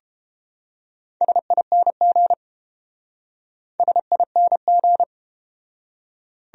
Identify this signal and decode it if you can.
Morse code, heard as a keyed tone, about 31 wpm, 700 Hz. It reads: SING SING